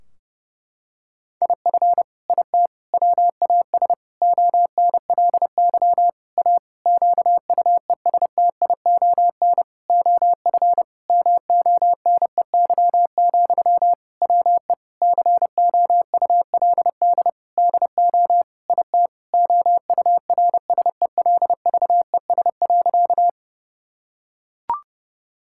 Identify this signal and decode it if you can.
Morse code, heard as a keyed tone, about 30 wpm, 700 Hz. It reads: IF IT WAS ONLY A QUESTION OF MONEY, WE COULD DO IT OURSELVES.
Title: Sentences from Top 1600 Words 30wpm